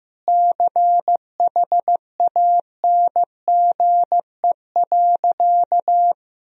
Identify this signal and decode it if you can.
Morse code, heard as a keyed tone, about 15 wpm, 700 Hz. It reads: CHANGE.
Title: Sentences from Top 1000 Words - Review Part 1 - 15wpm